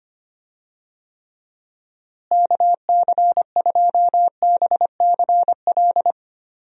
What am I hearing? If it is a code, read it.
Morse code, heard as a keyed tone, about 25 wpm, 700 Hz. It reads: KC2BCL
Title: USA Call Signs - 6 Characters - Repeated 2x - 25wpm